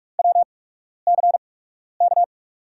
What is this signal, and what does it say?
Morse code, heard as a keyed tone, about 44 wpm, spaced as for 12 wpm, 700 Hz. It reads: WCK
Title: USA Call Signs - 5 Characters - 8x Character Spacing - 45wpm